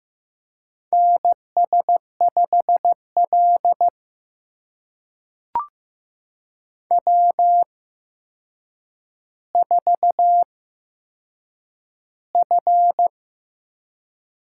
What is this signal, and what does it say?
Morse code, heard as a keyed tone, about 15 wpm, spaced as for 7 wpm, 700 Hz. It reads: NS5L W4F
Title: USA Call Signs - 8x Character Spacing - 15wpm